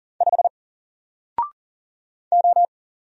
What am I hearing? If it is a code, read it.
Morse code, heard as a keyed tone, about 40 wpm, 700 Hz. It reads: 5 O
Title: Single Letter-Number Alt 40wpm